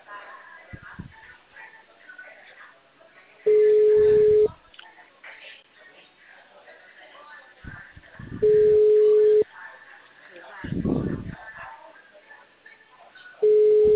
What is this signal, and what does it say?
Ringback tone (Europe, 425 Hz), with speech.
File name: out- -2014- - - .31860